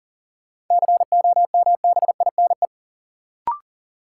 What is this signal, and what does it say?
Morse code, heard as a keyed tone, about 40 wpm, 700 Hz. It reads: COMBINE
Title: Sentences from Top 1600 Words - New Word Drill - 40wpm